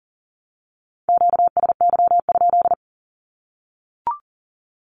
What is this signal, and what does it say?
Morse code, heard as a keyed tone, about 40 wpm, 700 Hz. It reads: QSY?
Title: Q-Codes 40wpm